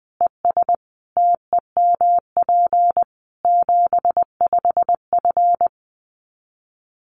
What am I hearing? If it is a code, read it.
Morse code, heard as a keyed tone, about 19 wpm, 700 Hz. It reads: ES TEMP 75F
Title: Common QSO Phrases 20wpm